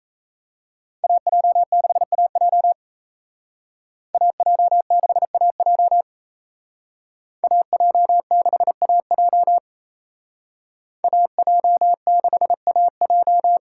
Morse code, heard as a keyed tone, about 33 wpm, 700 Hz. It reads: AJ6AJ AJ6AJ AJ6AJ AJ6AJ
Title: Speed Racing - USA Call Signs 28wpm